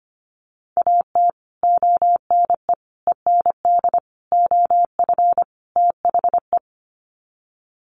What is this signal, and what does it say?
Morse code, heard as a keyed tone, about 25 wpm, 700 Hz. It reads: AT ONE END OF THE
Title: Sets of 5 Words Encore 25wpm